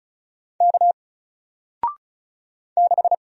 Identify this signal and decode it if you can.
Morse code, heard as a keyed tone, about 34 wpm, 700 Hz. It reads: K 6
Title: Single Character (Letter:Number:Punctuation:Prosign) 35wpm